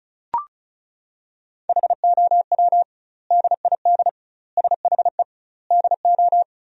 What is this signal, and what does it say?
Morse code, heard as a keyed tone, about 34 wpm, 700 Hz. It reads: HOW DID SHE DO
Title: 2x Word-Spacing - Sentences from Top 100 Words - 35wpm